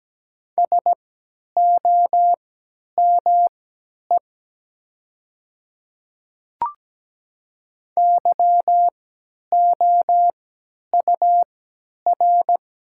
Morse code, heard as a keyed tone, about 17 wpm, spaced as for 10 wpm, 690 Hz. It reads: SOME YOUR
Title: 3x Character Spacing - Top 100 Words - 17wpm